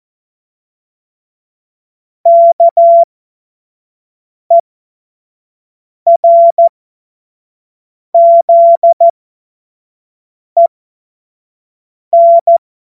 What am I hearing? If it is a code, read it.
Morse code, heard as a keyed tone, about 14 wpm, spaced as for 5 wpm, 680 Hz. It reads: KERZEN
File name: Morsekurs / Witze / Witz 14. September 2021, 25 F70